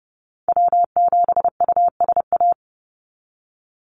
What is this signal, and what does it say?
Morse code, heard as a keyed tone, about 30 wpm, 700 Hz. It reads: W7USA
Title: NA SOTA Activator Callsigns 30wpm